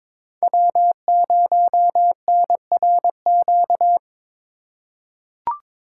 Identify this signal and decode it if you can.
Morse code, heard as a keyed tone, about 22 wpm, 700 Hz. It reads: W0NRQ
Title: USA Call Signs - 5 Characters - 2x Character Spacing - 22wpm